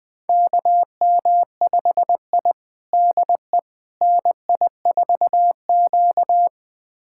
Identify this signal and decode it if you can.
Morse code, heard as a keyed tone, about 20 wpm, 700 Hz. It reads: KM5I DE NI4Q
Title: USA Callsign DE Callsign - 4 Characters - 20wpm